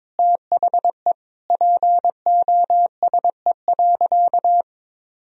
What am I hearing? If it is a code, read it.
Morse code, heard as a keyed tone, about 22 wpm, 700 Hz. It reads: THE POSE.
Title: Sentences from Top 1000 Words - Review Part 2 - 22wpm